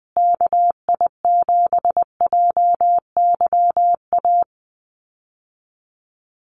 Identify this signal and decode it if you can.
Morse code, heard as a keyed tone, about 20 wpm, 700 Hz. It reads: KI7JYA
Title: USA Call Signs Repeated 3x 20wpm